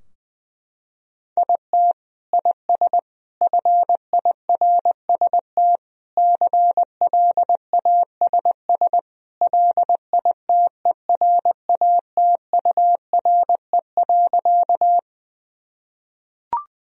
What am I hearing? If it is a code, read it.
Morse code, heard as a keyed tone, about 20 wpm, 700 Hz. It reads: IT IS FIRST CLASS LITERATURE.
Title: Sentences from Top 1000 Words - Review Part 3 - 20wpm